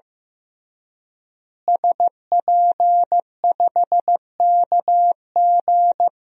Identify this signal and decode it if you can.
Morse code, heard as a keyed tone, about 15 wpm, 700 Hz. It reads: SP5KG
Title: DXCC-100 Call Signs - 5 Characters - Repeated 2x - 15wpm